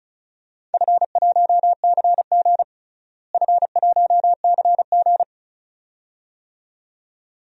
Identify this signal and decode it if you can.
Morse code, heard as a keyed tone, about 36 wpm, 700 Hz. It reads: F1CG F1CG
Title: DXCC-100 Call Signs - 4 Characters - Repeated 2x - 35wpm